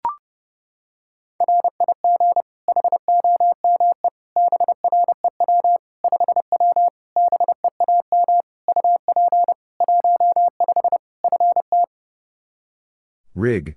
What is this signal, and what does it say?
Morse code, heard as a keyed tone, about 30 wpm, 700 Hz. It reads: RIG HOME BREW 5W BEAM UP 15 FT
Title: K1USN SST QSO Practice 30wpm